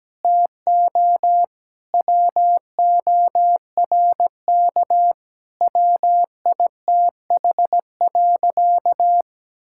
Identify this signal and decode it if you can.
Morse code, heard as a keyed tone, about 17 wpm, 700 Hz. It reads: TO WORK WITH.
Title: Sentences from Top 1000 Words - Review Part 5 - 17wpm